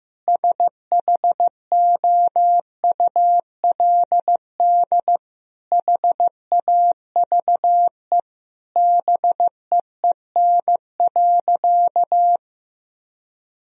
Morse code, heard as a keyed tone, about 15 wpm, 700 Hz. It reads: SHOULD HAVE BEEN.
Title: Sentences from Top 1000 Words - Review Part 4 - 15wpm